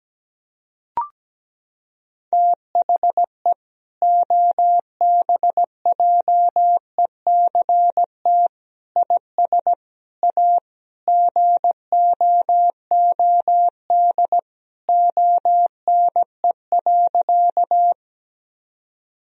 Morse code, heard as a keyed tone, about 17 wpm, 700 Hz. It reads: THE OBJECT IS A GOOD ONE.
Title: Sentences from Top 1000 Words - Review Part 3 - 17wpm